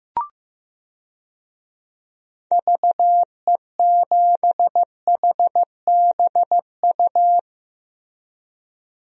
Morse code, heard as a keyed tone, about 15 wpm, 700 Hz. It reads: VE7HBU
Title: DXCC-100 Call Signs - 6 Characters - Repeated 2x - 15wpm